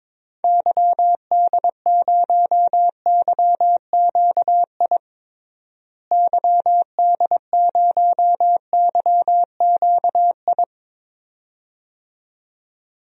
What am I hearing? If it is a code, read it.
Morse code, heard as a keyed tone, about 22 wpm, 700 Hz. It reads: YD0YQI YD0YQI
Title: DXCC-100 Call Signs - 6 Characters - Repeated 2x - 22wpm